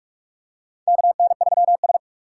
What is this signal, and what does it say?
Morse code, heard as a keyed tone, about 42 wpm, 700 Hz. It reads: KN3S
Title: USA Call Signs - 4 Characters - 4x Character Spacing - 45wpm